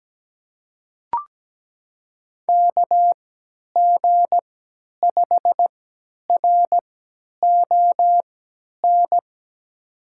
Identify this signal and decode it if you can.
Morse code, heard as a keyed tone, about 17 wpm, spaced as for 10 wpm, 700 Hz. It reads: KG5RON
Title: USA Call Signs - 3x Character Spacing - 17wpm